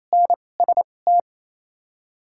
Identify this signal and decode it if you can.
Morse code, heard as a keyed tone, about 28 wpm, spaced as for 20 wpm, 700 Hz. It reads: NST
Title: 2x Character Spacing - Top 500 Words - 28wpm